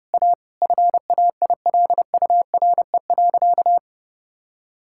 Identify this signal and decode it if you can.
Morse code, heard as a keyed tone, about 30 wpm, 700 Hz. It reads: A FAILURE.
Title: Sentences from Top 1000 Words - Review Part 6 - 30wpm